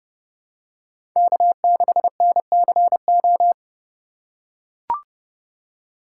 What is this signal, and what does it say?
Morse code, heard as a keyed tone, about 30 wpm, 700 Hz. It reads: K6NCO
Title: USA Call Signs - 5 Characters - 6x Character Spacing - 30wpm